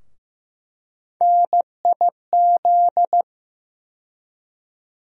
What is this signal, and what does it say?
Morse code, heard as a keyed tone, about 15 wpm, 700 Hz. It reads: NIZ
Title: USA Call Sign Postfixes - 3 Letters 15wpm